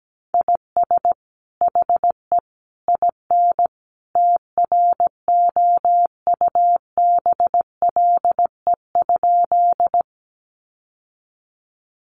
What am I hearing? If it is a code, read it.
Morse code, heard as a keyed tone, about 17 wpm, 700 Hz. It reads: IS HE IN TROUBLE?
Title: Sentences from Top 1200 Words 17wpm